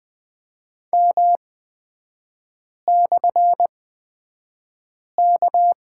Morse code, heard as a keyed tone, about 20 wpm, spaced as for 5 wpm, 700 Hz. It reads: M/K
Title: 224 Lesson 36e - BT - Three Characters 20wpm